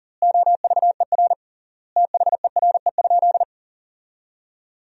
Morse code, heard as a keyed tone, about 40 wpm, 700 Hz. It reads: OVER THERE?
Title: 3x Word-Spacing - Sentences from Top 200 Words - 40wpm